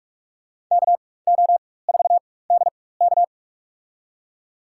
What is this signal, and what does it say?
Morse code, heard as a keyed tone, about 46 wpm, spaced as for 21 wpm, 700 Hz. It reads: KO4DK
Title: USA Call Signs - 5 Characters - 4x Character Spacing - 45wpm